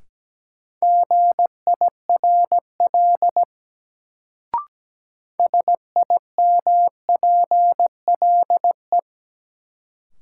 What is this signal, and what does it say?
Morse code, heard as a keyed tone, about 17 wpm, 700 Hz. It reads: GIRL SIMPLE
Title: Sentences from Top 400 Words - New Word Drill - 17wpm